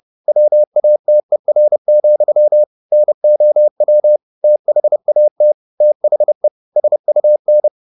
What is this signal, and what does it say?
Morse code, heard as a keyed tone, about 30 wpm, 600 Hz. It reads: WATER, NOW THAT THE SUN
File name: the old man and the sea / the old man and the sea 1 30wpm